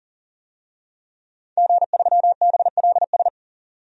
Morse code, heard as a keyed tone, about 40 wpm, 700 Hz. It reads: G3BLS
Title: DXCC-100 Call Signs - 5 Characters - 40wpm